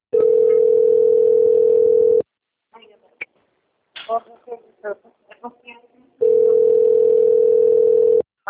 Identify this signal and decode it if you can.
Ringback tone (North America), 440 + 480 Hz, with speech.
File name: out- - - - - .26992